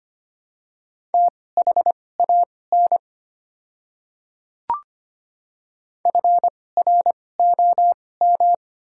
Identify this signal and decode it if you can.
Morse code, heard as a keyed tone, about 25 wpm, spaced as for 18 wpm, 700 Hz. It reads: THAN FROM
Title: 2x Character Spacing - Top 100 Words - 25wpm